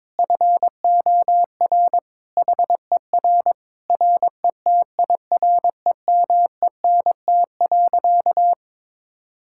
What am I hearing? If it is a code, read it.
Morse code, heard as a keyed tone, about 22 wpm, 700 Hz. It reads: FOR HER RETIREMENT.
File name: Difficult Sentences with Hints - 22wpm